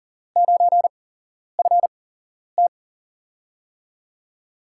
Morse code, heard as a keyed tone, about 42 wpm, spaced as for 11 wpm, 700 Hz. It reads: 9FT
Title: USA Call Signs - 8x Character Spacing - 40wpm